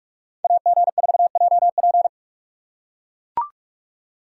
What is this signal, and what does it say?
Morse code, heard as a keyed tone, about 44 wpm, 700 Hz. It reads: AG4JP
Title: USA Call Signs - 5 Characters - 6x Character Spacing - 45wpm